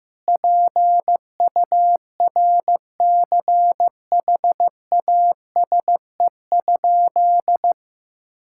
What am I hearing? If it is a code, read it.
Morse code, heard as a keyed tone, about 15 wpm, 700 Hz. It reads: PURCHASE?
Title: Sentences from Top 1000 Words - Review Part 5 - 15wpm